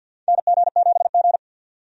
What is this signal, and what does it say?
Morse code, heard as a keyed tone, about 52 wpm, 700 Hz. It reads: NG7G
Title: USA Call Signs - 4 Characters - 50wpm